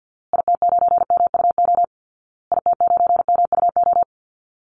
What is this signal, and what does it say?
Morse code, heard as a keyed tone, about 50 wpm, 700 Hz. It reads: IT9MUO IT9MUO
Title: Call Signs Repeated 2x - 50wpm